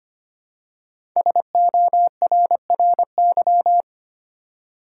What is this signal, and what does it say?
Morse code, heard as a keyed tone, about 25 wpm, 700 Hz. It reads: SORRY
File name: 5 Letter Words - First 3 Letters Given - 25wpm